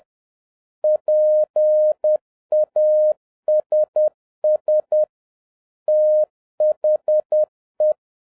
Morse code, heard as a keyed tone, about 10 wpm, 610 Hz. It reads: PASS THE